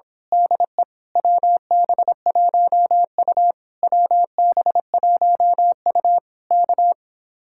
Morse code, heard as a keyed tone, about 26 wpm, 700 Hz. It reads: DE WB1U WB1U K
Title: Speed-Racing Common QSO Phrases Encore - 22wpm